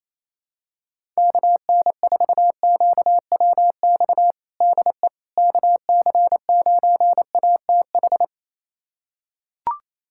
Morse code, heard as a keyed tone, about 28 wpm, 700 Hz. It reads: KN4QWX DE KC9ATH
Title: USA Callsign DE Callsign - 6 Characters - 28wpm